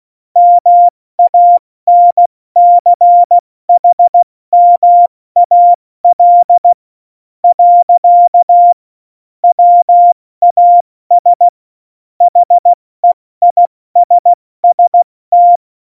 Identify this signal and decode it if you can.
Morse code, heard as a keyed tone, about 16 wpm, 700 Hz. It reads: MANCHMAL . WAS HEISST
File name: Morsekurs / Witze / Witz 03.Dezember 2019, 70 F80